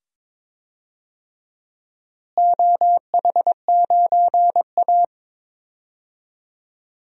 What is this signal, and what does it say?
Morse code, heard as a keyed tone, about 22 wpm, 700 Hz. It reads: OH9A
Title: DXCC-100 Call Signs - 4 Characters - Repeated 2x - 22wpm